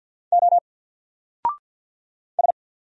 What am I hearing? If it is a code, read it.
Morse code, heard as a keyed tone, about 50 wpm, 700 Hz. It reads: O S
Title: Single Character (Letter:Number:Punctuation:Prosign) Alt 50wpm